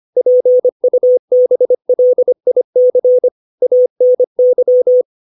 Morse code, heard as a keyed tone, about 25 wpm, 500 Hz. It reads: PUBLIC ANY
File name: 632 words / 632 Words 25WPM